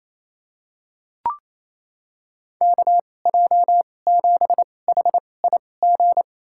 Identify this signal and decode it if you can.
Morse code, heard as a keyed tone, about 28 wpm, spaced as for 20 wpm, 700 Hz. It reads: KJ7HIG
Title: USA Call Signs - 6 Characters - 2x Character Spacing - 28wpm